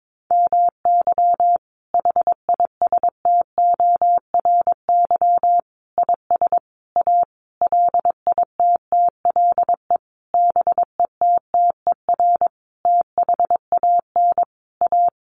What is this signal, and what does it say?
Morse code, heard as a keyed tone, about 22 wpm, 700 Hz. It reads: MY HISTORY IS A LITTLE BETTER THAN A
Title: Sentences from Top 1500 Words 22wpm